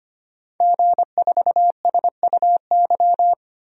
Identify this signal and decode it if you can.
Morse code, heard as a keyed tone, about 25 wpm, 700 Hz. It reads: G4SUY
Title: DXCC-100 Call Signs - 5 Characters - 25wpm